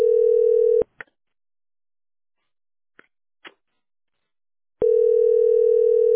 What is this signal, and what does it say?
Ringback tone (North America), 440 + 480 Hz.